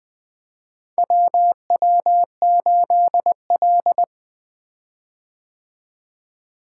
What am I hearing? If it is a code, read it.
Morse code, heard as a keyed tone, about 21 wpm, 700 Hz. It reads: WW8L
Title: 2023 POTA Activator Call Signs - 20wpm